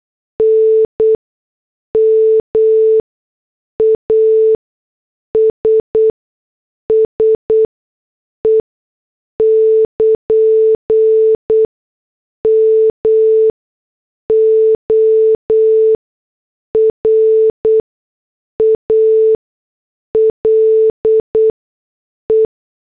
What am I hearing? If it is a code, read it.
Morse code, heard as a keyed tone, about 8 wpm, spaced as for 6 wpm, 440 Hz. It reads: N MASSE <KN> MORALE